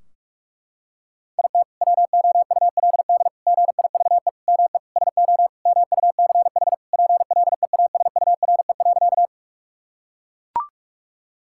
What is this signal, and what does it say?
Morse code, heard as a keyed tone, about 41 wpm, 700 Hz. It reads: IT WOULD GIVE ME SO MUCH PLEASURE.
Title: Sentences from Top 1600 Words 45wpm